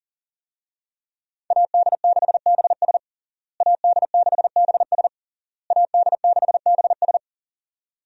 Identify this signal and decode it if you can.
Morse code, heard as a keyed tone, about 40 wpm, 700 Hz. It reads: AD6BS AD6BS AD6BS
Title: USA Call Signs - 5 Characters - Repeated 3x - 40wpm